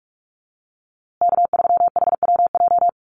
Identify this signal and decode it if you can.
Morse code, heard as a keyed tone, about 44 wpm, 700 Hz. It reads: K3HWJ
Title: USA Call Signs Repeated 3x 45wpm